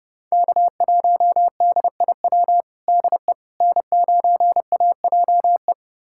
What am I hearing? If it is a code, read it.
Morse code, heard as a keyed tone, about 30 wpm, 700 Hz. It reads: K1DIW DE N9AJE
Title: USA Callsign DE Callsign - 5 Characters - 30wpm